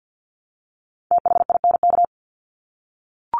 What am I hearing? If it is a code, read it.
Morse code, heard as a keyed tone, about 50 wpm, 700 Hz. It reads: THINK
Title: Top 200 Words Repeated 50wpm